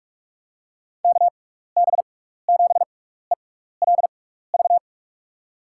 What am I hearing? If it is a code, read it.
Morse code, heard as a keyed tone, about 46 wpm, spaced as for 16 wpm, 700 Hz. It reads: KB7ELV
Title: USA Call Signs - 6 Characters - 6x Character Spacing - 45wpm